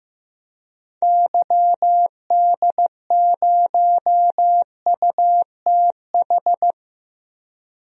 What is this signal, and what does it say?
Morse code, heard as a keyed tone, about 15 wpm, 700 Hz. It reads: YD0UTH
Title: DXCC-100 Call Signs - 6 Characters - 15wpm